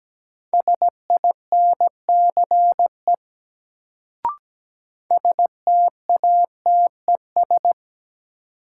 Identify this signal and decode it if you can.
Morse code, heard as a keyed tone, about 17 wpm, 700 Hz. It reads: SINCE STATES
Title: Sets of 2 Words - New Word Drill - 17wpm